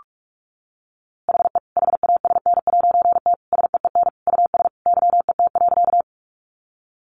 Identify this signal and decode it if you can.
Morse code, heard as a keyed tone, about 44 wpm, 700 Hz. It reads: HE HASN'T SEEN US YET.
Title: A Sound Of Thunder 45wpm